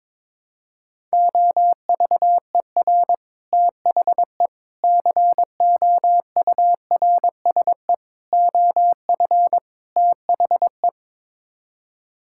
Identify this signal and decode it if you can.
Morse code, heard as a keyed tone, about 22 wpm, 700 Hz. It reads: OVER THE COURSE OF THE
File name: practice / 22wpm / Build A Phrase - 5-Word Phrases - 22wpm